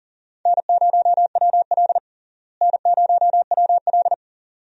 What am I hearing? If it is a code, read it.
Morse code, heard as a keyed tone, about 40 wpm, 700 Hz. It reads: N0WL N0WL
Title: USA Call Signs - 4 Characters - Repeated 2x - 40wpm